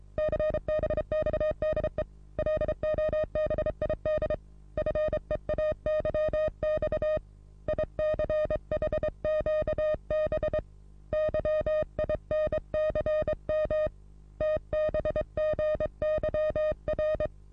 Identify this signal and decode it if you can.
Morse code, heard as a keyed tone, about 30 wpm, 620 Hz. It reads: CBXDE LOBID FEAYX ICHQB YINCM TBGYR
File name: repos / Morsekurs / DJ4UF Morsen27